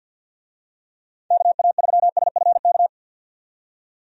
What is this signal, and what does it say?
Morse code, heard as a keyed tone, about 52 wpm, 700 Hz. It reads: KA3SFK
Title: USA Call Signs - 6 Characters - 50wpm